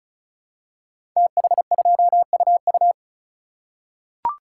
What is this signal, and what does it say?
Morse code, heard as a keyed tone, about 36 wpm, 700 Hz. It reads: TH2UU
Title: Code Groups - 5 Letter:Numbers - 35wpm